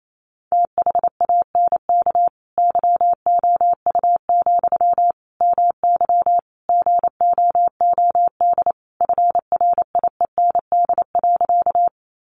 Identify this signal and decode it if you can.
Morse code, heard as a keyed tone, about 28 wpm, 700 Hz. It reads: THANK YOU, MY GOOD FRIEND.
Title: Sentences from Top 600 Words 28wpm